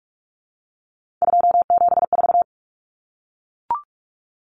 Morse code, heard as a keyed tone, about 44 wpm, 700 Hz. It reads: 274
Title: Numbers - 3-Digit 45wpm